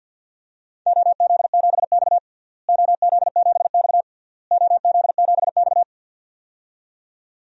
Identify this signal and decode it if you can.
Morse code, heard as a keyed tone, about 53 wpm, 700 Hz. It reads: OZ7X OZ7X OZ7X
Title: DXCC-100 Call Signs - 4 Characters - Repeated 3x - 50wpm